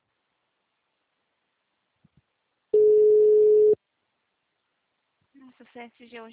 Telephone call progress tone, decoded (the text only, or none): ringback tone